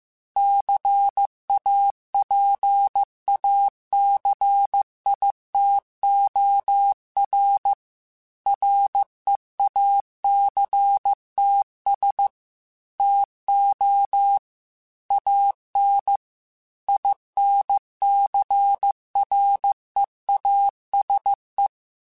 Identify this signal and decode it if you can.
Morse code, heard as a keyed tone, about 15 wpm, 800 Hz. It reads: CAPACITOR REACTS TO AN INCREASE